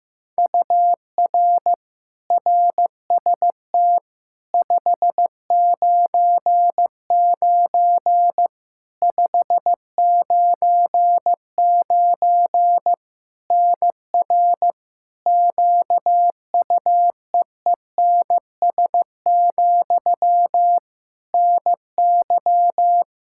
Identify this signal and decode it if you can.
Morse code, heard as a keyed tone, about 15 wpm, 700 Hz. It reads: UR RST 599 599 NR QUEENS, NY
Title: K1USN SST QSO Practice 15wpm